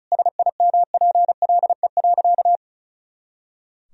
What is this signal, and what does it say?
Morse code, heard as a keyed tone, about 36 wpm, 700 Hz. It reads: SIMPLE.
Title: Sentences from Top 500 Words and Abbreviations 35wpm